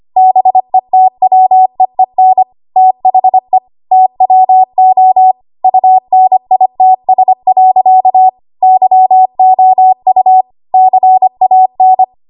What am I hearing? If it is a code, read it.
Morse code, heard as a keyed tone, about 25 wpm, 750 Hz. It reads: BETWEEN THE TWO UNITS. YOU CAN
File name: Archive / 25 WPM / 25WPM